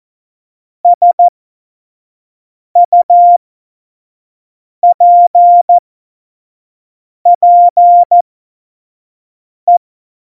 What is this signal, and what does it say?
Morse code, heard as a keyed tone, about 14 wpm, spaced as for 5 wpm, 700 Hz. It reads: SUPPE
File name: Morsekurs / Witze / Witz 02. Juli 2019, 25 F70